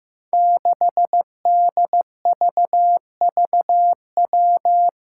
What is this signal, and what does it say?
Morse code, heard as a keyed tone, about 15 wpm, 700 Hz. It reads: 6DVVW
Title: Five Challenging Letter-Number Combinations 15wpm